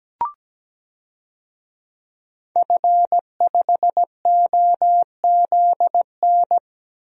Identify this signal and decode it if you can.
Morse code, heard as a keyed tone, about 17 wpm, 700 Hz. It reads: F5OZN
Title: DXCC-100 Call Signs - 5 Characters - Repeated 3x - 17wpm